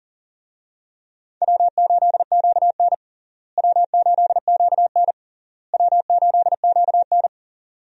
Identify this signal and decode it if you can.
Morse code, heard as a keyed tone, about 40 wpm, 700 Hz. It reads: W8QN W8QN W8QN
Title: USA Call Signs - 4 Characters - Repeated 3x - 40wpm